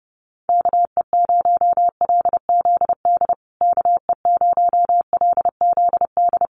Morse code, heard as a keyed tone, about 30 wpm, 700 Hz. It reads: KE0LZD KE0LZD
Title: USA Call Signs Repeated 2x 30wpm